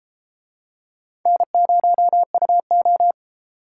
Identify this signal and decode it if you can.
Morse code, heard as a keyed tone, about 33 wpm, 700 Hz. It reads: N0UO